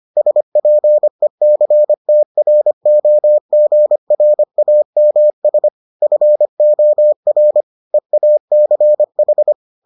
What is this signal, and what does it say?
Morse code, heard as a keyed tone, about 25 wpm, 600 Hz. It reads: SPECTROGRAMS FOR EACH